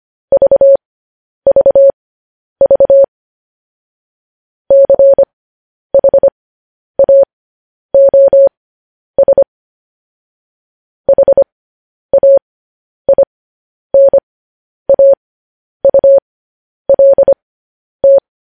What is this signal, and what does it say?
Morse code, heard as a keyed tone, about 25 wpm, spaced as for 10 wpm, 550 Hz. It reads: VVV CHAOS HAINAULT
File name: podcast / News headlines-1x-25x10